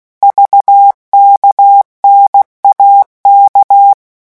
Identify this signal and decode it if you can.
Morse code, heard as a keyed tone, about 16 wpm, 800 Hz. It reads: VKNAK
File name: morse mp3 / revision5-vitesse 15 mots